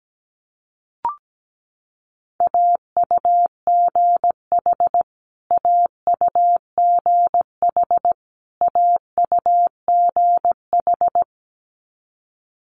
Morse code, heard as a keyed tone, about 17 wpm, 700 Hz. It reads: AUGH AUGH AUGH
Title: Multi-letter Phonemes Repeated 3x 17wpm